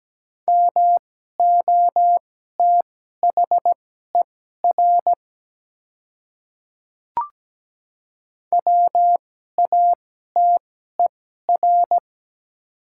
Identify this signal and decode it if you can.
Morse code, heard as a keyed tone, about 17 wpm, spaced as for 12 wpm, 700 Hz. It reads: MOTHER WATER
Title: 2x Character Spacing - Top 200 Words - 17wpm